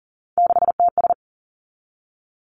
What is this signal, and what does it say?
Morse code, heard as a keyed tone, about 39 wpm, 700 Hz. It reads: 6TS
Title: Three Challenging Letter-Number Combinations 40wpm